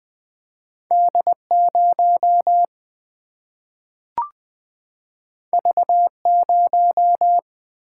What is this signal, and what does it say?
Morse code, heard as a keyed tone, about 20 wpm, 700 Hz. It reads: D0 V0
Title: Single-Letter Single-Number 20wpm